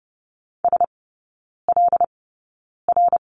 Morse code, heard as a keyed tone, about 30 wpm, spaced as for 9 wpm, 700 Hz. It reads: SLR
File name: 081 Lesson 14e - Number 2 - Three Characters 30wpm